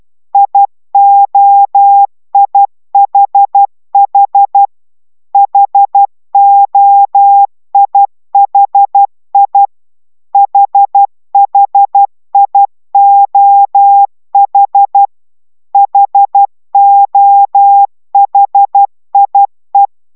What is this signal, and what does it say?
Morse code, heard as a keyed tone, about 12 wpm, 800 Hz. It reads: IOIHH HOIHI HHIOH HOHIE